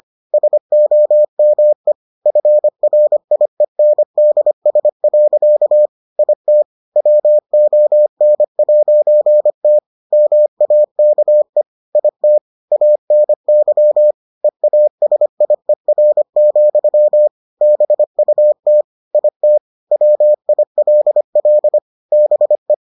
Morse code, heard as a keyed tone, about 25 wpm, 600 Hz. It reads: SOME FRIENDS. IT WON'T MAKE IT ANY EASIER, BUT IT WILL BE